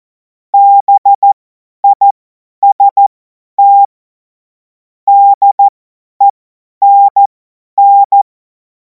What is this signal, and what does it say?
Morse code, heard as a keyed tone, about 14 wpm, spaced as for 10 wpm, 800 Hz. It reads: BIST DENN
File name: Morsekurs / Witze / Witz 30. Juni 2026 50 F70